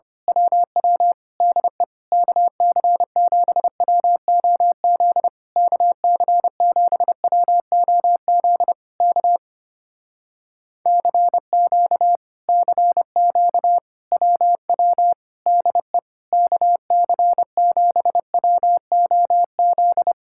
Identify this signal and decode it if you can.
Morse code, heard as a keyed tone, about 26 wpm, 700 Hz. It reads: WW DE KC7WOZ KC7WOZ K CQ CQ WW DE KC7WOZ
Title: Speed-Racing Common QSO Phrases - 25wpm